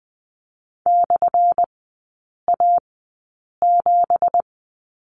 Morse code, heard as a keyed tone, about 20 wpm, spaced as for 8 wpm, 690 Hz. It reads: /A7
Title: 198 Lesson 32e - Letter k - Three Characters 20wpm